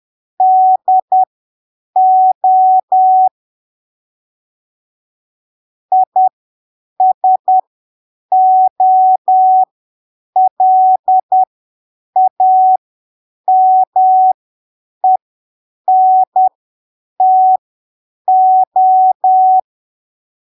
Morse code, heard as a keyed tone, about 10 wpm, spaced as for 7 wpm, 750 Hz. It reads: DO ISOLAMENTO
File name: arquivos / texto / texto 23 10ppm